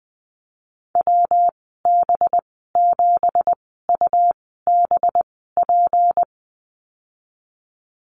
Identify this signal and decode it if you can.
Morse code, heard as a keyed tone, about 19 wpm, spaced as for 14 wpm, 700 Hz. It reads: WB7UBP
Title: 186 Lesson 30g - Letter b - Call Signs 20wpm